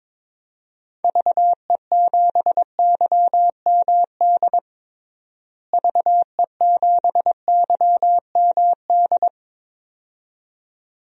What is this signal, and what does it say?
Morse code, heard as a keyed tone, about 22 wpm, 700 Hz. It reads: VE7YMD VE7YMD
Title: DXCC-100 Call Signs - 6 Characters - Repeated 2x - 22wpm